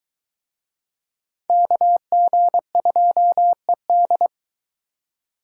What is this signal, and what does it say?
Morse code, heard as a keyed tone, about 21 wpm, 700 Hz. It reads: KG2ED